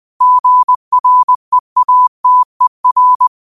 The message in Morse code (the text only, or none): GREATER